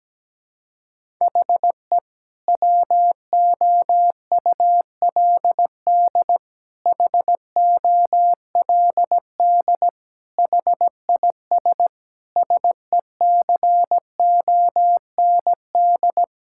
Morse code, heard as a keyed tone, about 17 wpm, 700 Hz. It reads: HE WOULD HOLD HIS SECOND
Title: Sentences from Top 1000 Words - Review Part 6 - 17wpm